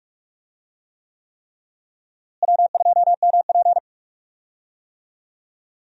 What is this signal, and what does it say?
Morse code, heard as a keyed tone, about 46 wpm, 700 Hz. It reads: W2MP
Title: USA Call Signs - 4 Characters - Repeated 3x - 45wpm